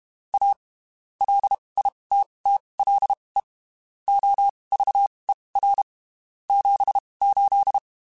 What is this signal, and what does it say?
Morse code, heard as a keyed tone, about 32 wpm, spaced as for 23 wpm, 800 Hz. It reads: A LITTLE OVER 78
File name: (spaced) / QSOs Spaced-mp3s / kc7sac @30 sp